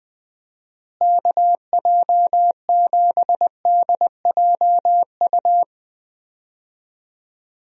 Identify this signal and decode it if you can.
Morse code, heard as a keyed tone, about 21 wpm, 700 Hz. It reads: KJ7DJU
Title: USA Call Signs - 6 Characters - Repeated 3x - 20wpm